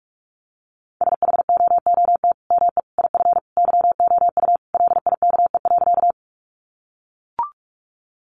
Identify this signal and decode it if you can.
Morse code, heard as a keyed tone, about 44 wpm, 700 Hz. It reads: SHOOT ME IF YOU LIKE.
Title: Sentences from Top 1200 Words 45wpm